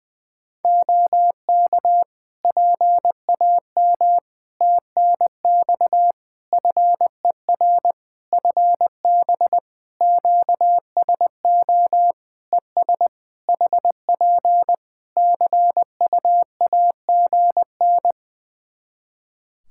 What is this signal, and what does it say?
Morse code, heard as a keyed tone, about 20 wpm, 700 Hz. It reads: OK PAM TNX FER FB QSO ES HP CUAGN
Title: K1USN SST QSO Practice 20wpm